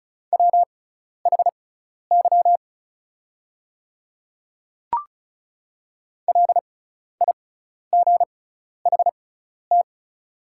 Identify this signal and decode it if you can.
Morse code, heard as a keyed tone, about 34 wpm, spaced as for 12 wpm, 690 Hz. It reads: WHY LIGHT
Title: 6x Character Spacing - Top 200 Words - 35wpm